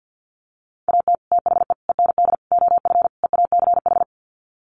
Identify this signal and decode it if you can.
Morse code, heard as a keyed tone, about 50 wpm, 700 Hz. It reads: AT THE END OF EACH
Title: Sets of 5 Words Encore 50wpm